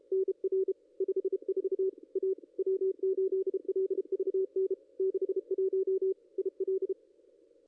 Morse code, heard as a keyed tone, about 31 wpm, 380 Hz. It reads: NR 54 A W8LVN 61 IL
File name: contests / ss-cw-k5tr-op-n5rz -2200